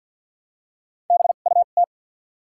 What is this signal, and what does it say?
Morse code, heard as a keyed tone, about 54 wpm, 700 Hz. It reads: B U T
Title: 2x Character Spacing - Top 100 Words - 50wpm